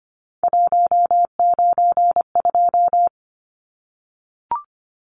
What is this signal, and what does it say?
Morse code, heard as a keyed tone, about 25 wpm, 700 Hz. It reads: 192
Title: Numbers - 3-Digit 25wpm